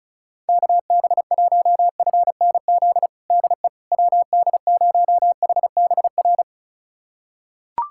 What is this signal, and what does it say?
Morse code, heard as a keyed tone, about 36 wpm, 700 Hz. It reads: KB1FNZ DE WD0HBR
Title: USA Callsign DE Callsign - 6 Characters - 35wpm